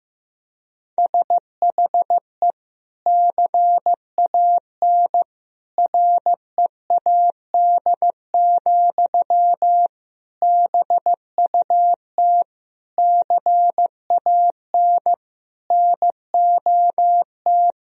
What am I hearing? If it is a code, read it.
Morse code, heard as a keyed tone, about 15 wpm, 700 Hz. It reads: SHE CAN READ, BUT CAN NOT
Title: Sentences from Top 1000 Words - Review Part 1 - 15wpm